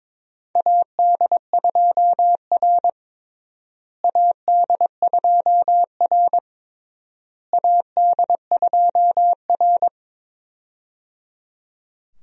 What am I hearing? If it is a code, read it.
Morse code, heard as a keyed tone, about 22 wpm, 700 Hz. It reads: AD2R AD2R AD2R
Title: USA Call Signs - 4 Characters - Repeated 3x - 22wpm